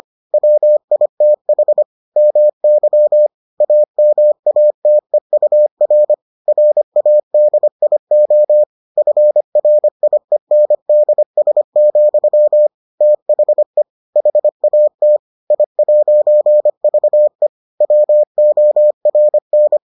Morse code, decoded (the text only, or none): WITH MY AMATEUR RADIO FRIENDS, THE HAT I'VE WORN